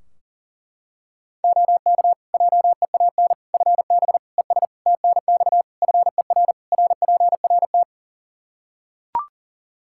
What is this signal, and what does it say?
Morse code, heard as a keyed tone, about 40 wpm, 700 Hz. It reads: OK JEAN FB ES TNX FER RPRT
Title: Practice QSO - Exchange 2 - Info - 40wpm